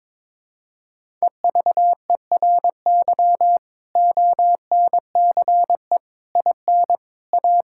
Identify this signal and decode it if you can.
Morse code, heard as a keyed tone, about 21 wpm, 700 Hz. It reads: EVERY ONCE IN A
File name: practice / 22wpm / Build A Phrase - 5-Word Phrases - 22wpm